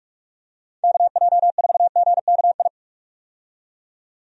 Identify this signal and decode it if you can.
Morse code, heard as a keyed tone, about 48 wpm, 700 Hz. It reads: KJ4GKI
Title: USA Call Signs - 6 Characters - 45wpm